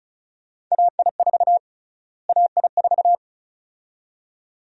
Morse code, heard as a keyed tone, about 36 wpm, 700 Hz. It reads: AI4 AI4
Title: USA Call Sign Prefixes - Repeated 2x - 35wpm